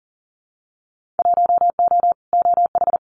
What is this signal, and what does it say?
Morse code, heard as a keyed tone, about 41 wpm, 700 Hz. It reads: 1O OH
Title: Winter Field Day Exchanges 40wpm